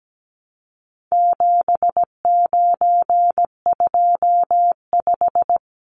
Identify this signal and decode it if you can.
Morse code, heard as a keyed tone, about 17 wpm, 700 Hz. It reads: 7925
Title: Numbers - 4-Digit 17wpm